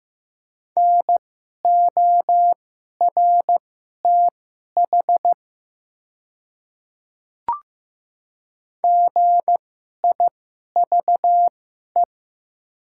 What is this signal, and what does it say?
Morse code, heard as a keyed tone, about 15 wpm, spaced as for 11 wpm, 700 Hz. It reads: NORTH GIVE
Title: 2x Character Spacing - Top 300 Words - 15wpm